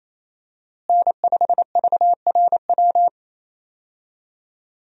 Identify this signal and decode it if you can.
Morse code, heard as a keyed tone, about 28 wpm, 700 Hz. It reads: N5VRW
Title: USA Call Signs - 5 Characters - Repeated 3x - 28wpm